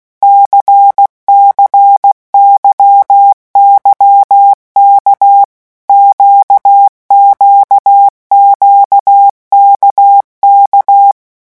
Morse code, heard as a keyed tone, about 16 wpm, 800 Hz. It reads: CCYYK QQQKK
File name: morse mp3 / revision1-vitesse 15 mots